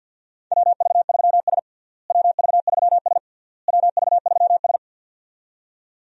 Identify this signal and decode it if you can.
Morse code, heard as a keyed tone, about 53 wpm, 700 Hz. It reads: WV3S WV3S WV3S
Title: USA Call Signs - 4 Characters - Repeated 3x - 50wpm